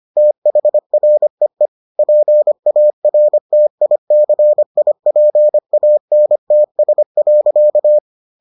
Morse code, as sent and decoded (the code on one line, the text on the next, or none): - .... .-. . / .--. .- .-. - .. -.-. .. .--. .- -. - ... .-.-.-
THREE PARTICIPANTS.